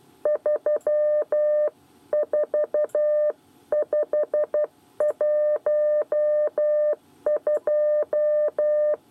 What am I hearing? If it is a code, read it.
Morse code, heard as a keyed tone, about 12 wpm, 600 Hz. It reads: 34512
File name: 8 bis lezione cw 1